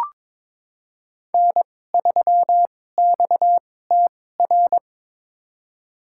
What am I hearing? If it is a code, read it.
Morse code, heard as a keyed tone, about 22 wpm, spaced as for 16 wpm, 700 Hz. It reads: N3XTR
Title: USA Call Signs - 2x Character Spacing - 22wpm